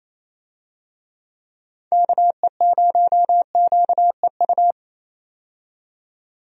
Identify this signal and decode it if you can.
Morse code, heard as a keyed tone, about 27 wpm, 700 Hz. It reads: KE0QEU
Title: USA Call Signs - 6 Characters - 28wpm